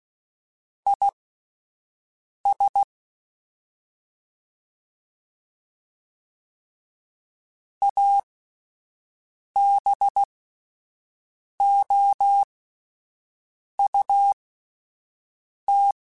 Morse code, heard as a keyed tone, about 16 wpm, spaced as for 5 wpm, 800 Hz. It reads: IS ABOUT
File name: Spaced Characters / Spaced Characters-mp3s / Weather4 21 sp @10